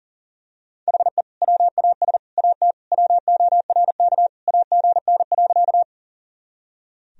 Morse code, heard as a keyed tone, about 40 wpm, 700 Hz. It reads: HE WAS AT WORK AGN.
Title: Sentences from Top 200 Words and Abbreviations 40wpm